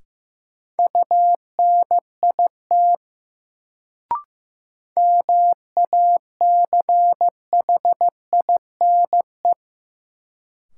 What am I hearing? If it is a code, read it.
Morse code, heard as a keyed tone, about 15 wpm, 700 Hz. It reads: UNIT MACHINE
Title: Sentences from Top 1000 Words - Review Part 2 - New Word Drill - 15wpm